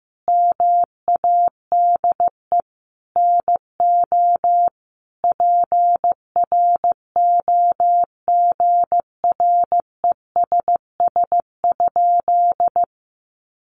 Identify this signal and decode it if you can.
Morse code, heard as a keyed tone, about 15 wpm, 700 Hz. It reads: MADE NO PROGRESS?
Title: Sentences from Top 1400 Words 15wpm